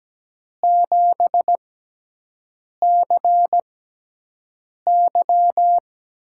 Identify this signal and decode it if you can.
Morse code, heard as a keyed tone, about 17 wpm, spaced as for 6 wpm, 700 Hz. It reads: 7CY
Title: USA Call Signs - 6 Characters - 6x Character Spacing - 17wpm